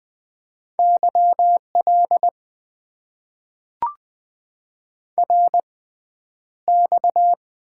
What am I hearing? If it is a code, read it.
Morse code, heard as a keyed tone, about 20 wpm, spaced as for 10 wpm, 700 Hz. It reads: YL RX
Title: Top 90 QSO Elements - 6x Character Spacing - 20wpm